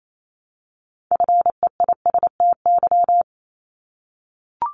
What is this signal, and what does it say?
Morse code, heard as a keyed tone, about 28 wpm, 700 Hz. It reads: FEISTY
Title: Multi-letter Phonemes Repeated 3x 28wpm